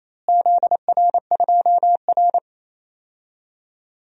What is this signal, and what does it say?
Morse code, heard as a keyed tone, about 27 wpm, 700 Hz. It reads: ZR2R
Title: DXCC-100 Call Signs - 4 Characters - Repeated 3x - 28wpm